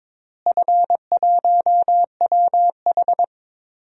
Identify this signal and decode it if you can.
Morse code, heard as a keyed tone, about 22 wpm, 700 Hz. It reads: F1WH
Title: DXCC-100 Call Signs - 4 Characters - Repeated 3x - 22wpm